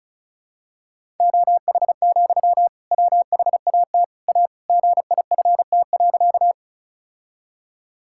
Morse code, heard as a keyed tone, about 36 wpm, 700 Hz. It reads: OH, WHAT A GIFT.
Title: Sentences from Top 1000 Words - Review Part 6 - 35wpm